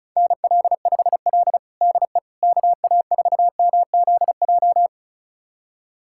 Morse code, heard as a keyed tone, about 36 wpm, 700 Hz. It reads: NL5L DE KA4MZJ
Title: USA Callsign DE Callsign 35wpm